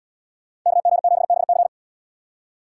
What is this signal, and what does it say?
Morse code, heard as a keyed tone, about 77 wpm, 700 Hz. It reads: KG7BC